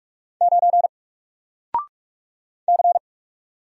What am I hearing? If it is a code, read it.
Morse code, heard as a keyed tone, about 47 wpm, 700 Hz. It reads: 9 C
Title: Single Character (Letter:Number:Punctuation:Prosign) Alt 45wpm